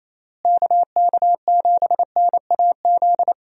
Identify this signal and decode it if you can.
Morse code, heard as a keyed tone, about 28 wpm, 700 Hz. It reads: KK7NAZ
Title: USA Call Signs - 6 Characters - Repeated 3x - 28wpm